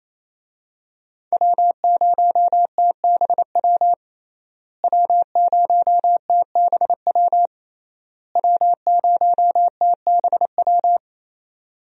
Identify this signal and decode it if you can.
Morse code, heard as a keyed tone, about 27 wpm, 700 Hz. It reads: W0TBW W0TBW W0TBW
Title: USA Call Signs - 5 Characters - Repeated 3x - 28wpm